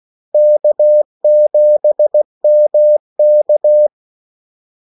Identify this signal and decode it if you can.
Morse code, heard as a keyed tone, about 16 wpm, 600 Hz. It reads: K7MK
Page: Kaldesignaler 16 wpm